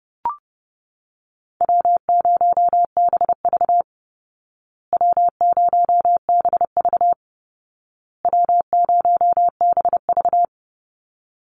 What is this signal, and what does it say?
Morse code, heard as a keyed tone, about 30 wpm, 700 Hz. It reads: W0BV W0BV W0BV
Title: Easy Call Signs Repeated 3x - 30wpm